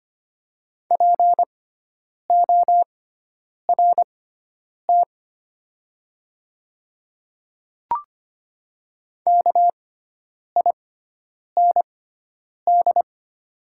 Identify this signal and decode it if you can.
Morse code, heard as a keyed tone, about 25 wpm, spaced as for 9 wpm, 700 Hz. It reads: PORT KIND
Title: 6x Character Spacing - Top 300 Words - 25wpm